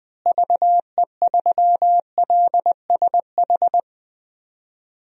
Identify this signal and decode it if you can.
Morse code, heard as a keyed tone, about 20 wpm, 700 Hz. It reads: VE3LSH
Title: DXCC-100 Call Signs - 6 Characters - Repeated 3x - 20wpm